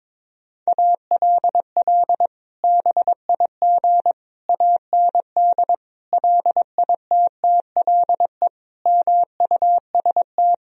Morse code, heard as a keyed tone, about 22 wpm, 700 Hz. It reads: ALL BIG AND LITTLE MUST
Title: Sentences from Top 1000 Words - Review Part 1 - 22wpm